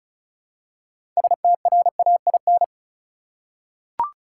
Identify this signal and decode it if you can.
Morse code, heard as a keyed tone, about 36 wpm, 700 Hz. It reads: STRAIN
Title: Six Letter Words - 35wpm